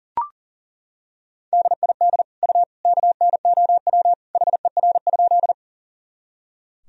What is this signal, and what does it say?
Morse code, heard as a keyed tone, about 40 wpm, 700 Hz. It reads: DID U KNOW HER?
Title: Sentences from Top 100 Words and Abbreviations 40wpm